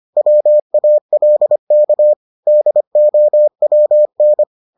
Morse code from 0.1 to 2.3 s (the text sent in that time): WALK